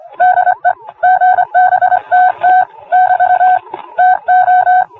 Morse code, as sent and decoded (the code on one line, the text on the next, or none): -.. / --. -... - - / -...- / - ---
DE GBTT <BT> TO